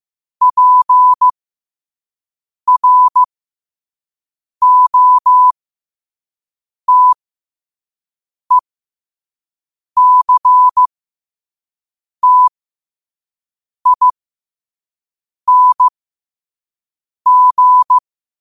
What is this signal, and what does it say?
Morse code, heard as a keyed tone, about 15 wpm, spaced as for 5 wpm, 1000 Hz. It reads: PROTECTING
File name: qotdfiles / 5wpm